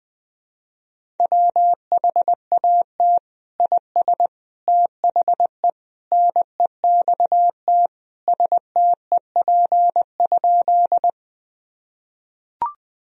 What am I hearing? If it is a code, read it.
Morse code, heard as a keyed tone, about 21 wpm, 700 Hz. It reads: WHAT IS THE NEXT STEP?
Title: Sentences from Top 1000 Words - Review Part 2 - 20wpm